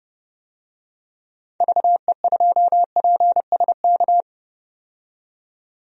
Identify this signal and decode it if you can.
Morse code, heard as a keyed tone, about 30 wpm, 700 Hz. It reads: VE2PSK
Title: DXCC-100 Call Signs - 6 Characters - 30wpm